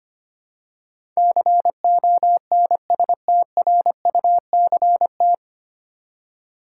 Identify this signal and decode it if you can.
Morse code, heard as a keyed tone, about 25 wpm, 700 Hz. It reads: CONSTRUCT
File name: Nine Letter Words - 25wpm